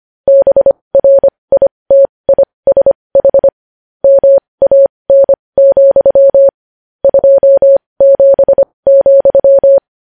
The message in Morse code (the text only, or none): BRITISH MAN, 27,